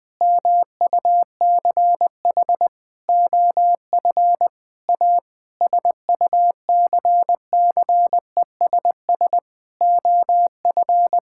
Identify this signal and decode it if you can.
Morse code, heard as a keyed tone, about 20 wpm, 700 Hz. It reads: MUCH OF A SUCCESS OF
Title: Sentences from Top 1000 Words - Review Part 6 - 20wpm